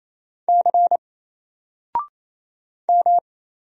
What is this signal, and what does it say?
Morse code, heard as a keyed tone, about 28 wpm, 700 Hz. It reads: C M
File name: Single-Letter-Number-28wpm